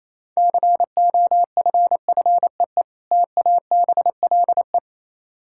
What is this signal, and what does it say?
Morse code, heard as a keyed tone, about 27 wpm, 700 Hz. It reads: COFFEE TABLE
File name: Easy Words - Speed-Racing with Hints - 22wpm